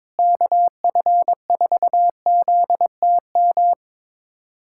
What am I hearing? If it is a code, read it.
Morse code, heard as a keyed tone, about 22 wpm, 700 Hz. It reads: KF4ZTM
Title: USA Callsign DE Callsign - 6 Characters - 22wpm